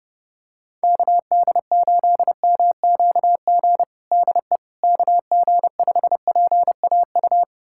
Morse code, heard as a keyed tone, about 30 wpm, 700 Hz. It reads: KD8MQG DE KG5PAU
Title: USA Callsign DE Callsign 30wpm